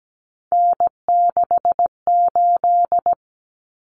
Morse code, heard as a keyed tone, about 17 wpm, 690 Hz. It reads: N68
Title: Common Cut Numbers - 3-Digit 17wpm